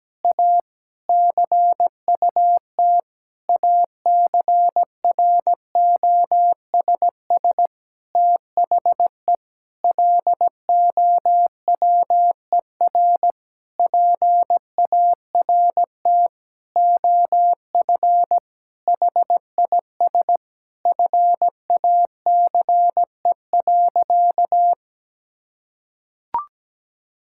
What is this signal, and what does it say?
Morse code, heard as a keyed tone, about 17 wpm, 700 Hz. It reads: A CUT ACROSS THE LOWER PART OF HIS FACE.
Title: Sentences from Top 1000 Words - Review Part 5 - 17wpm